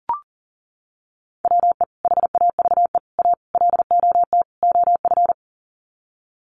Morse code, heard as a keyed tone, about 40 wpm, 700 Hz. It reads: WE HAVE A LOT OF
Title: Sets of 5 Words 40wpm